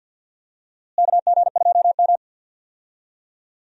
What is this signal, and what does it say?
Morse code, heard as a keyed tone, about 52 wpm, 700 Hz. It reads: KG2M
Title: USA Call Signs - 4 Characters - Repeated 2x - 50wpm